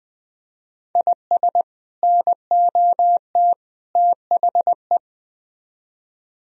Morse code, heard as a keyed tone, about 20 wpm, 700 Hz. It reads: IS NOT THE
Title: Speed-Racing - Sets of 3 Words 17wpm